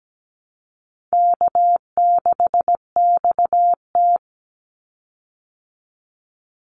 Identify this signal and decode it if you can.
Morse code, heard as a keyed tone, about 17 wpm, 700 Hz. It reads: K6XT
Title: Easy Call Signs Repeated 3x - 17wpm